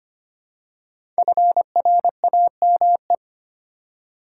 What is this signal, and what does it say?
Morse code, heard as a keyed tone, about 25 wpm, 700 Hz. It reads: FRAME
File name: 5 Letter Words - First 3 Letters Given - 25wpm